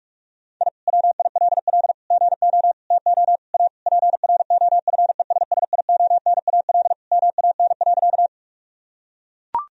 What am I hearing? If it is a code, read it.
Morse code, heard as a keyed tone, about 42 wpm, 700 Hz. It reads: I WILL GO TO A PROFESSIONAL MAN.
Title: Sentences from Top 1000 Words - Review Part 4 - 45wpm